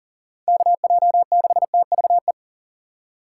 Morse code, heard as a keyed tone, about 39 wpm, 700 Hz. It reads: KJ6TVE
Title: USA Call Signs - 6 Characters - 3x Character Spacing - 40wpm